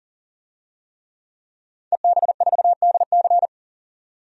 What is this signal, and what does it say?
Morse code, heard as a keyed tone, about 40 wpm, 700 Hz. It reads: EB4DC
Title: DXCC-100 Call Signs - 5 Characters - 40wpm